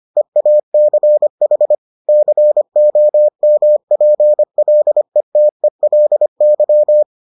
Morse code, heard as a keyed tone, about 25 wpm, 600 Hz. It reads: EACH COMPLETELY